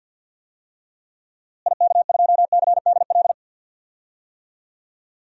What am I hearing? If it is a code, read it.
Morse code, heard as a keyed tone, about 52 wpm, 700 Hz. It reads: IK2CDL
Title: DXCC-100 Call Signs - 6 Characters - Repeated 3x - 50wpm